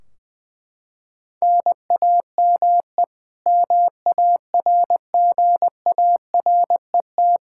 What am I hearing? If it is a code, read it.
Morse code, heard as a keyed tone, about 20 wpm, 700 Hz. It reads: NAME MARGARET
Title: Practice QSO - Exchange 1 - Beginning - 20wpm